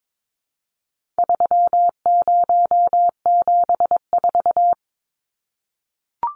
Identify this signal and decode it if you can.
Morse code, heard as a keyed tone, about 22 wpm, 700 Hz. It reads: 3074
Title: Numbers - 4-Digit 22wpm